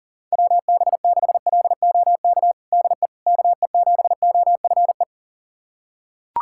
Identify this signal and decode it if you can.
Morse code, heard as a keyed tone, about 40 wpm, 700 Hz. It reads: WB6LOK DE KE7OFE
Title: USA Callsign DE Callsign - 6 Characters - 40wpm